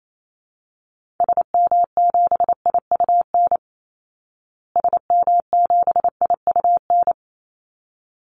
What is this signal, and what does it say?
Morse code, heard as a keyed tone, about 28 wpm, 700 Hz. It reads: SM7IUN SM7IUN
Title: Call Signs Repeated 3x - 28wpm